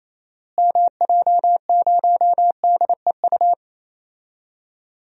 Morse code, heard as a keyed tone, about 28 wpm, 700 Hz. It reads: MJ0DEU
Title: DXCC-100 Call Signs - 6 Characters - Repeated 2x - 28wpm